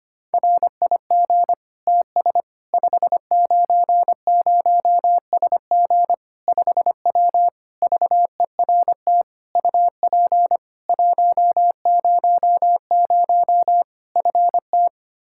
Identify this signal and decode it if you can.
Morse code, heard as a keyed tone, about 25 wpm, 700 Hz. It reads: RIG TS 590SG 5W VERT UP 100 FT
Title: Practice QSO - Exchange 2 - Info - 25wpm